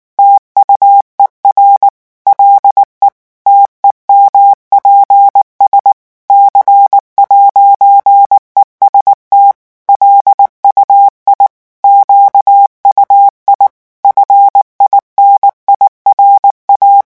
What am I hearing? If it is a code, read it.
Morse code, heard as a keyed tone, about 19 wpm, 800 Hz. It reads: TUER LE TEMPS C'EST LUI QUI FINIRA
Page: REVISION 8 - 8 mots minute